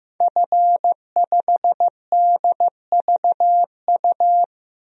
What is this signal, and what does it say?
Morse code, heard as a keyed tone, about 15 wpm, 700 Hz. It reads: F5DVU
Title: DXCC-100 Call Signs - 5 Characters - 15wpm